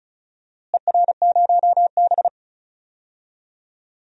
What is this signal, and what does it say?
Morse code, heard as a keyed tone, about 34 wpm, 700 Hz. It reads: ER0B
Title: DXCC-100 Call Signs - 4 Characters - Repeated 3x - 35wpm